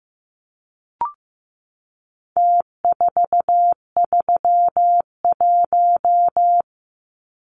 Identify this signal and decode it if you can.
Morse code, heard as a keyed tone, about 15 wpm, 700 Hz. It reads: T431
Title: Numbers - Contest Style - 4-Digit 15wpm